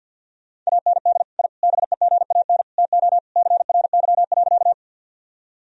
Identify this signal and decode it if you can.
Morse code, heard as a keyed tone, about 53 wpm, 700 Hz. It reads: AND I BEGAN TO CRY.
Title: Sentences from Top 1000 Words - Review Part 2 - 50wpm